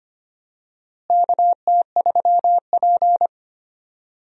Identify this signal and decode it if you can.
Morse code, heard as a keyed tone, about 25 wpm, 700 Hz. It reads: KT3P
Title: USA Call Signs - 4 Characters - Repeated 2x - 25wpm